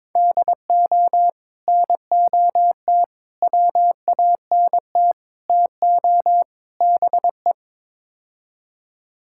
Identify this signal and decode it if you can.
Morse code, heard as a keyed tone, about 22 wpm, 700 Hz. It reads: DO NOT WANT TO BE
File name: practice / 22wpm / Build A Phrase - 5-Word Phrases - 22wpm